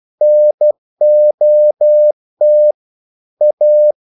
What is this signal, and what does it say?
Morse code, heard as a keyed tone, about 12 wpm, 600 Hz. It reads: NOT A
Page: English quotations in CW (telegraphy - morse)